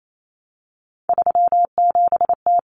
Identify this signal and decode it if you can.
Morse code, heard as a keyed tone, about 28 wpm, 700 Hz. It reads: 37T
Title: Common Cut Numbers - 3-Digit 28wpm